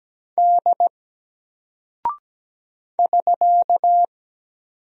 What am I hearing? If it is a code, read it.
Morse code, heard as a keyed tone, about 17 wpm, 700 Hz. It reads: D <SK>
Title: Single Character (Letter:Number:Punctuation:Prosign) 17wpm